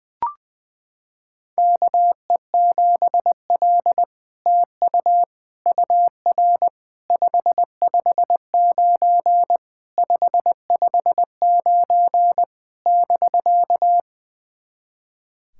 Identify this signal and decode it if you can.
Morse code, heard as a keyed tone, about 20 wpm, 700 Hz. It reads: KE7L TU UR 559 559 <BK>
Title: SOTA QSO Practice 20wpm